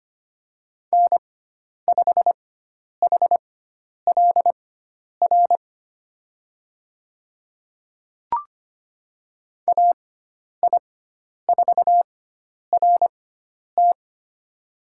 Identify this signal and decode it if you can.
Morse code, heard as a keyed tone, about 25 wpm, spaced as for 10 wpm, 700 Hz. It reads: N5HLR AI4RT
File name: Lesson 16g - Letter c - Call Signs 25wpm